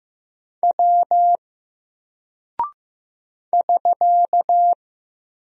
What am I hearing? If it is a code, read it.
Morse code, heard as a keyed tone, about 15 wpm, 700 Hz. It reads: W <SK>
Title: Single Character (Letter:Number:Punctuation:Prosign) 15wpm